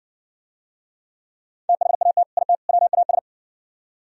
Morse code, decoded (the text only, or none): THAT IT WAS